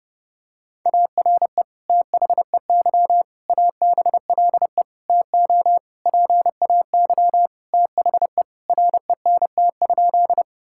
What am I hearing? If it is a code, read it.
Morse code, heard as a keyed tone, about 32 wpm, 700 Hz. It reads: ARE THEY ABLE TO PAY THE RENT?
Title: Sentences from Top 1000 Words - Review Part 5 - 30wpm